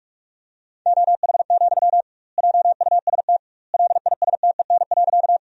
Morse code, heard as a keyed tone, about 46 wpm, 700 Hz. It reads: OH, JUST LISTEN.
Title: 2x Word-Spacing - Sentences from Top 500 Words - 45wpm